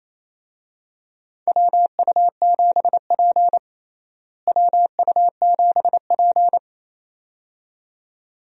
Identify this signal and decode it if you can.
Morse code, heard as a keyed tone, about 28 wpm, 700 Hz. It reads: WU7P WU7P
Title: USA Call Signs - 4 Characters - Repeated 2x - 28wpm